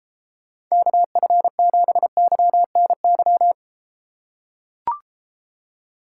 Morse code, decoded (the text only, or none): KF7YNY